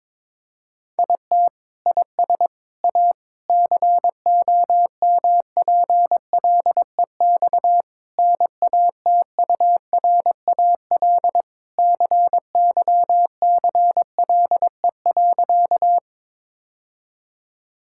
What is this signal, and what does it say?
Morse code, heard as a keyed tone, about 22 wpm, 700 Hz. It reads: IT IS A COMPLEX NATURAL CYCLE.
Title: Sentences from Top 1000 Words - Review Part 6 - 22wpm